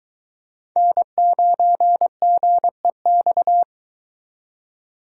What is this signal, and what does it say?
Morse code, heard as a keyed tone, about 22 wpm, 700 Hz. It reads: N9GEX